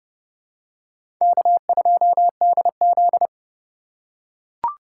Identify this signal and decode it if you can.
Morse code, heard as a keyed tone, about 32 wpm, 700 Hz. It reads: K2DZ
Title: USA Call Signs - 4 Characters - 4x Character Spacing - 30wpm